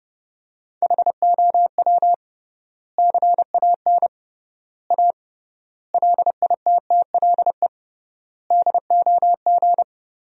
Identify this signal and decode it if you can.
Morse code, heard as a keyed tone, about 32 wpm, 700 Hz. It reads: HOW CAN A LITTLE DOG
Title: 3x Word-Spacing - Sentences from Top 500 Words - 30wpm